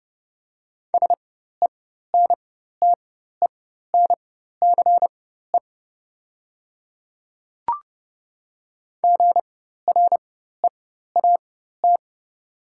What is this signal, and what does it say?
Morse code, heard as a keyed tone, about 29 wpm, spaced as for 14 wpm, 700 Hz. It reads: SENTENCE GREAT
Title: 4x Character Spacing - Top 200 Words - 30wpm